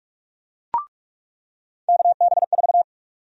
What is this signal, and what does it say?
Morse code, heard as a keyed tone, about 48 wpm, 700 Hz. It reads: KB4
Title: USA Call Sign Prefixes - 45wpm